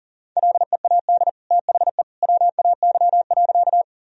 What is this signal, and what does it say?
Morse code, heard as a keyed tone, about 40 wpm, 700 Hz. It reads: LEAD THE WAY.
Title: Sentences from Top 1000 Words - Review Part 2 - 40wpm